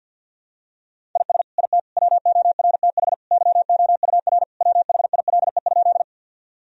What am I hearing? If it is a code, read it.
Morse code, heard as a keyed tone, about 50 wpm, 700 Hz. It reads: IS IT WORTH YOUR WHILE?
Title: Sentences from Top 1000 Words - Review Part 1 - 50wpm